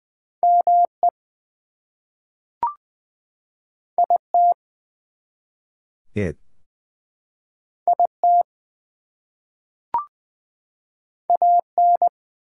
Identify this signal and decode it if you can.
Morse code, heard as a keyed tone, about 20 wpm, 700 Hz. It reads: ME IT IT AN
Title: Two Letter Words - 20wpm